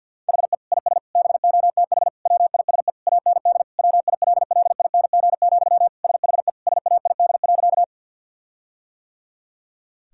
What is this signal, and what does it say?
Morse code, heard as a keyed tone, about 50 wpm, 700 Hz. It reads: HE IS BOTH WISE AND WILLING, SHE SAID.
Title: Sentences from Top 1600 Words 50wpm